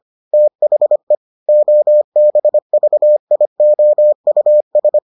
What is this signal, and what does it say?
Morse code, heard as a keyed tone, about 25 wpm, 600 Hz. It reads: THE OBVIOUS